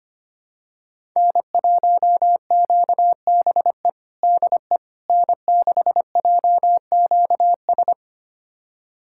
Morse code, heard as a keyed tone, about 25 wpm, 700 Hz. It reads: N1QBE DE N6JQS
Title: USA Callsign DE Callsign - 5 Characters - 25wpm